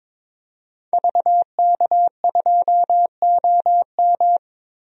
Morse code, heard as a keyed tone, about 22 wpm, 700 Hz. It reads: VK2OM
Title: DXCC-100 Call Signs - 5 Characters - 22wpm